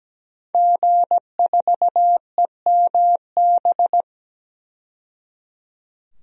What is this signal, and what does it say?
Morse code, heard as a keyed tone, about 17 wpm, 700 Hz. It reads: G4EMB
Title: DXCC-100 Call Signs - 5 Characters - 17wpm